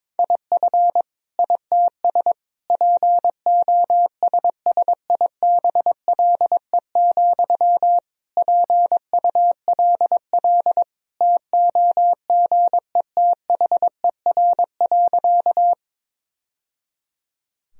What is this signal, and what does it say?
Morse code, heard as a keyed tone, about 22 wpm, 700 Hz. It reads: IF ITS POSSIBLE, PULL TOGETHER.
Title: Sentences from Top 1000 Words - Review Part 2 - 22wpm